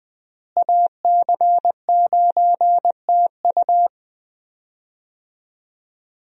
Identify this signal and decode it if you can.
Morse code, heard as a keyed tone, about 20 wpm, 700 Hz. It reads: AC9TU
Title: USA Call Signs - 5 Characters - Repeated 2x - 20wpm